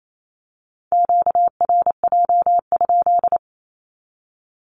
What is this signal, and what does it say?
Morse code, heard as a keyed tone, about 28 wpm, 700 Hz. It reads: QRJ?
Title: Q-Codes 28wpm